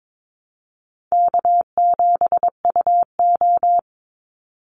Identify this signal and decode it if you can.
Morse code, heard as a keyed tone, about 22 wpm, 700 Hz. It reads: K7UO
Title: USA Call Signs 22wpm